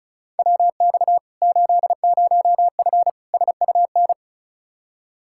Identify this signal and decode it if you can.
Morse code, heard as a keyed tone, about 36 wpm, 700 Hz. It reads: WX 80F SUN
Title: Practice QSO - Exchange 2 - Info - 35wpm